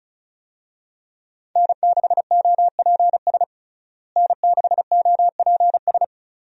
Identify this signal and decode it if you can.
Morse code, heard as a keyed tone, about 34 wpm, 700 Hz. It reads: N6OPS N6OPS
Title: USA Call Signs - 5 Characters - Repeated 2x - 35wpm